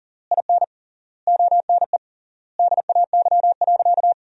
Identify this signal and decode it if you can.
Morse code, heard as a keyed tone, about 40 wpm, 700 Hz. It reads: IN ONE DAY.
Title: 3x Word-Spacing - Sentences from Top 500 Words - 40wpm